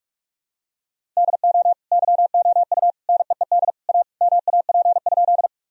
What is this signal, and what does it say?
Morse code, heard as a keyed tone, about 42 wpm, 700 Hz. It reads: DO YOU NEED A MAP?
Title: Sentences from Top 1000 Words - Review Part 2 - 45wpm